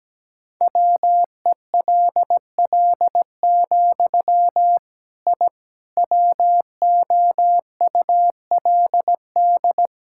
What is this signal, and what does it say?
Morse code, heard as a keyed tone, about 17 wpm, 700 Hz. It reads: WELL, I WOULD
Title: Sentences from Top 1000 Words - Review Part 5 - 17wpm